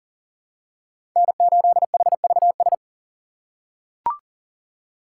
Morse code, heard as a keyed tone, about 40 wpm, 700 Hz. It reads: N8HVS
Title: USA Call Signs - 5 Characters - 3x Character Spacing - 40wpm